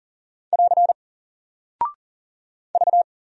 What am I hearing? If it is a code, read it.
Morse code, heard as a keyed tone, about 40 wpm, 700 Hz. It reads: <AR> V
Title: Single Character (Letter:Number:Punctuation:Prosign) Alt 40wpm